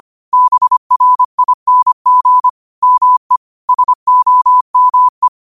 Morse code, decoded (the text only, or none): BRING ME SOME